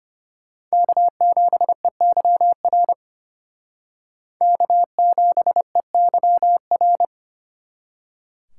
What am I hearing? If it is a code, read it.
Morse code, heard as a keyed tone, about 32 wpm, 700 Hz. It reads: K7EYR K7EYR